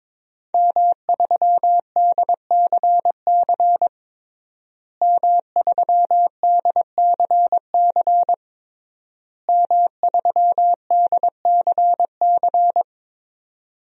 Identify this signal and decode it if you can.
Morse code, heard as a keyed tone, about 22 wpm, 700 Hz. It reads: M3DCC M3DCC M3DCC
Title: DXCC-100 Call Signs - 5 Characters - Repeated 3x - 22wpm